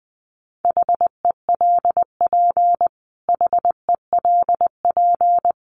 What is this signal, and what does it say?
Morse code, heard as a keyed tone, about 20 wpm, 700 Hz. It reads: HELP HELP
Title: Top 200 Words Repeated 20wpm